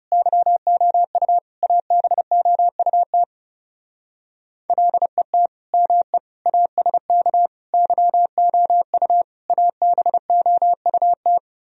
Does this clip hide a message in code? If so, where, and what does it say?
Yes: Morse code from 0.1 to 11.4 s, heard as a keyed tone, about 38 wpm, 700 Hz. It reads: YOU ABOUT LET ME ASK YOU ABOUT